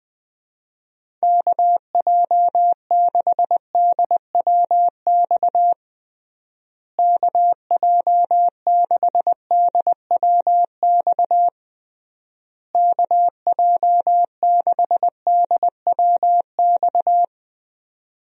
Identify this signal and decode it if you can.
Morse code, heard as a keyed tone, about 21 wpm, 700 Hz. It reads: KJ6DWX KJ6DWX KJ6DWX
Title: USA Call Signs - 6 Characters - Repeated 3x - 20wpm